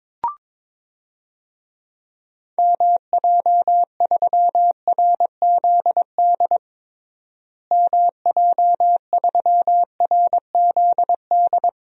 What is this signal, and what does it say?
Morse code, heard as a keyed tone, about 22 wpm, 700 Hz. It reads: MJ3RZD MJ3RZD
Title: DXCC-100 Call Signs - 6 Characters - Repeated 3x - 22wpm